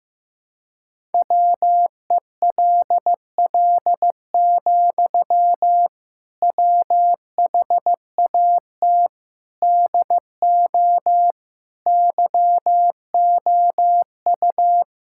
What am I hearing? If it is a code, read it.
Morse code, heard as a keyed tone, about 15 wpm, 700 Hz. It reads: WELL, WHAT DO YOU
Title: Sentences from Top 1600 Words 15wpm